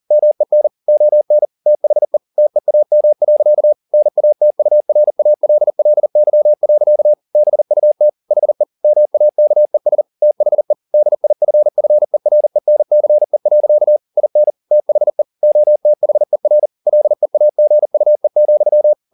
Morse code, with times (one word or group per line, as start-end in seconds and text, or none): 0.1-0.7 MEN
0.9-1.5 ON
1.7-2.2 THE
2.4-3.7 TEAM.
3.9-7.1 NATURALLY.
7.4-8.1 BUT
8.3-8.6 HE
8.8-10.0 MAKES
10.2-10.7 THE
10.9-14.0 DIFFERENCE.
14.2-14.5 IN
14.7-15.2 THE
15.4-16.7 OTHER
16.9-18.9 LEAGUE,